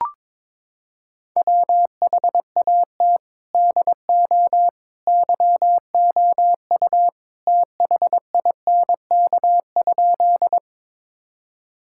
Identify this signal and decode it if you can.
Morse code, heard as a keyed tone, about 22 wpm, 700 Hz. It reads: WHAT DO YOU THINK?
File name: practice / 22wpm / Sentences with 2-Word Phrases - 22wpm